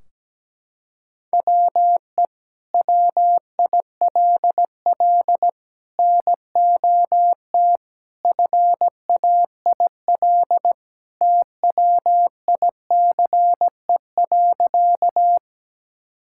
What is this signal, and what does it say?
Morse code, heard as a keyed tone, about 17 wpm, 700 Hz. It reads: WE WILL NOT FAIL TWICE.
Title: Sentences from Top 1600 Words 17wpm